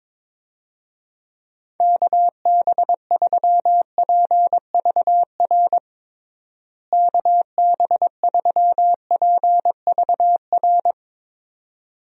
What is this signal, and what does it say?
Morse code, heard as a keyed tone, about 22 wpm, 700 Hz. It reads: KB3PVR KB3PVR
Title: USA Call Signs - 6 Characters - Repeated 2x - 22wpm